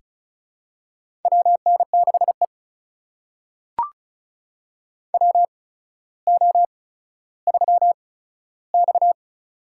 Morse code, heard as a keyed tone, about 36 wpm, spaced as for 15 wpm, 700 Hz. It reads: WN6E WO3X
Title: USA Call Signs - 8x Character Spacing - 35wpm